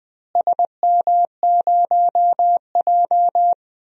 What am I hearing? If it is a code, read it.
Morse code, heard as a keyed tone, about 20 wpm, 700 Hz. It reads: SM0J
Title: DXCC-100 Call Signs - 4 Characters - Repeated 3x - 20wpm